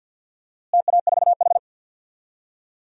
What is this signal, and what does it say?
Morse code, heard as a keyed tone, about 50 wpm, 700 Hz. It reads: TA4H
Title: DXCC-100 Call Signs - 4 Characters - Repeated 3x - 50wpm